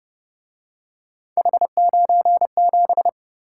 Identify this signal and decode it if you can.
Morse code, heard as a keyed tone, about 29 wpm, 700 Hz. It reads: H97